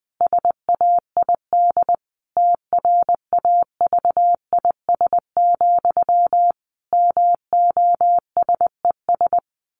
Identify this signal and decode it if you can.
Morse code, heard as a keyed tone, about 20 wpm, 700 Hz. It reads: SAID TRAVIS, MOSES
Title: A Sound Of Thunder 20wpm